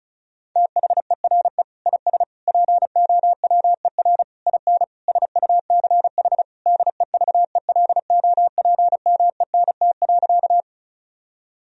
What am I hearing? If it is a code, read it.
Morse code, heard as a keyed tone, about 36 wpm, 700 Hz. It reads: THERE IS POWER IN SUCH DEVELOPMENT.
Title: Sentences from Top 1000 Words - Review Part 4 - 35wpm